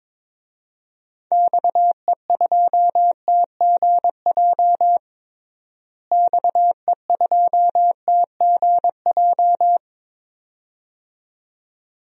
Morse code, heard as a keyed tone, about 22 wpm, 700 Hz. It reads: XE2TGJ XE2TGJ
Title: DXCC-100 Call Signs - 6 Characters - Repeated 2x - 22wpm